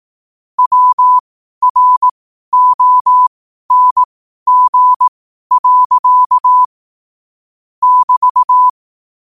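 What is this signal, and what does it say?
Morse code, heard as a keyed tone, about 18 wpm, spaced as for 12 wpm, 1000 Hz. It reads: WRONG. <BT>
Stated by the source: Morse practice sent at 12 wpm.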